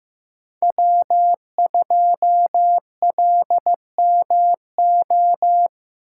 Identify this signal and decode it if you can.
Morse code, heard as a keyed tone, about 15 wpm, 700 Hz. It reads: W2LMO
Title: USA Call Signs - 8x Character Spacing - 15wpm